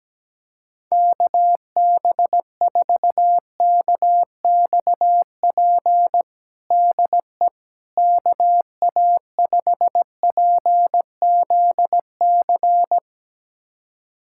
Morse code, heard as a keyed tone, about 17 wpm, 700 Hz. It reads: KB4KXP DE KA5PZC
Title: USA Callsign DE Callsign - 6 Characters - 17wpm